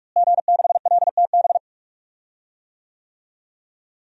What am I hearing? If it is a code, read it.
Morse code, heard as a keyed tone, about 41 wpm, 700 Hz. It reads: G6LTB
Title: DXCC-100 Call Signs - 5 Characters - Repeated 2x - 45wpm